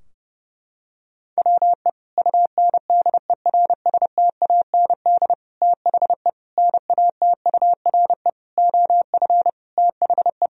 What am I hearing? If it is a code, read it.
Morse code, heard as a keyed tone, about 32 wpm, 700 Hz. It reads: WE UNDERSTAND THE NATURE OF THE
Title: Sentences from Top 1000 Words - Review Part 2 - 30wpm